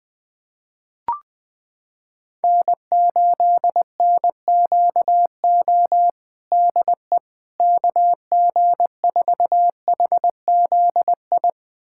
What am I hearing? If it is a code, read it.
Morse code, heard as a keyed tone, about 20 wpm, 700 Hz. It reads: N8NQO DE KG4HZI
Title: USA Callsign DE Callsign 20wpm